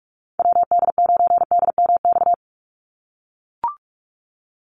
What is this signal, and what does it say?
Morse code, heard as a keyed tone, about 44 wpm, 700 Hz. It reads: WD9DMX
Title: USA Call Signs 45wpm